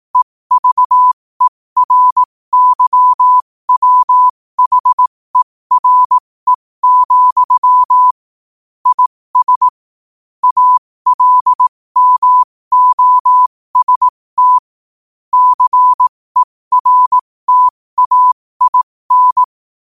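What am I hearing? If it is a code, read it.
Morse code, heard as a keyed tone, about 18 wpm, spaced as for 15 wpm, 1000 Hz. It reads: EVERYWHERE, IS ALMOST CERTAIN
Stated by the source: Morse code quote of the day sent at 15 words per minute.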